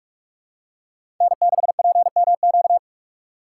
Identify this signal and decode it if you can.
Morse code, heard as a keyed tone, about 47 wpm, 700 Hz. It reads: N6PMQ
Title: USA Call Signs - 5 Characters - 6x Character Spacing - 45wpm